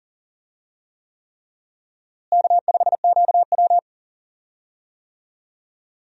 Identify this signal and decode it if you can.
Morse code, heard as a keyed tone, about 40 wpm, 700 Hz. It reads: K5QW
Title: USA Call Signs - 4 Characters - Repeated 3x - 40wpm